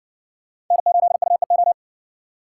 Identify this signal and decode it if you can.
Morse code, heard as a keyed tone, about 60 wpm, 700 Hz. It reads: N8UEO